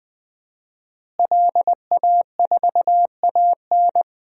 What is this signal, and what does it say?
Morse code, heard as a keyed tone, about 20 wpm, 700 Hz. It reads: LA4AN
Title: DXCC-100 Call Signs - 5 Characters - Repeated 2x - 20wpm